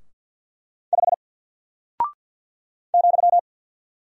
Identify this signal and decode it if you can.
Morse code, heard as a keyed tone, about 50 wpm, 700 Hz. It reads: 5 ,
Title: Single Character (Letter:Number:Punctuation:Prosign) Alt 50wpm